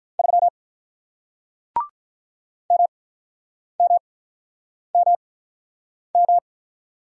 Morse code, heard as a keyed tone, about 48 wpm, spaced as for 9 wpm, 700 Hz. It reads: 3 MMMM